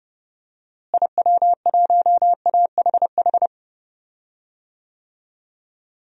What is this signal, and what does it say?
Morse code, heard as a keyed tone, about 30 wpm, 700 Hz. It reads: IW1AHH
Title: DXCC-100 Call Signs - 6 Characters - Repeated 2x - 30wpm